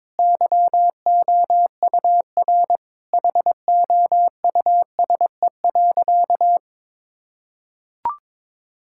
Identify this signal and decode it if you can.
Morse code, heard as a keyed tone, about 22 wpm, 700 Hz. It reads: YOUR HOUSE.
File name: practice / 22wpm / Sentences with 2-Word Phrases - 22wpm